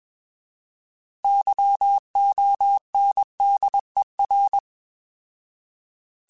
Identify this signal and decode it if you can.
Morse code, heard as a keyed tone, about 21 wpm, 800 Hz. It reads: YONDER